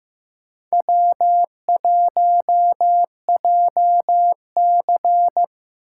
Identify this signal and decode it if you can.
Morse code, heard as a keyed tone, about 18 wpm, 700 Hz. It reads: W1JC
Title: Speed Racing - USA 1x2 Call Signs 15wpm